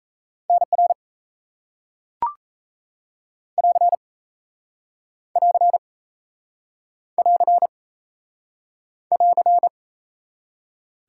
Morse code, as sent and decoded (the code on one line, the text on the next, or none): -. .-. / .-.-. / .-.-. / .-.-. / .-.-.
NR <AR> <AR> <AR> <AR>